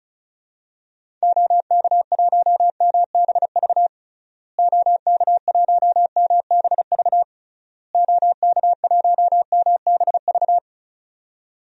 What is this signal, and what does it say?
Morse code, heard as a keyed tone, about 36 wpm, 700 Hz. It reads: OK1MBV OK1MBV OK1MBV
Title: DXCC-100 Call Signs - 6 Characters - Repeated 3x - 35wpm